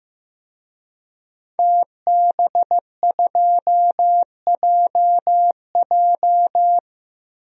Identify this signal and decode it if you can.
Morse code, heard as a keyed tone, about 15 wpm, 700 Hz. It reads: TB2JJ
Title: DXCC-100 Call Signs - 5 Characters - 15wpm